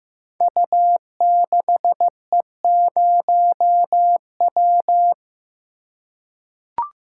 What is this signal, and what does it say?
Morse code, heard as a keyed tone, about 15 wpm, 700 Hz. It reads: U6E0W
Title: Code Groups - 5 Letter:Numbers - 15wpm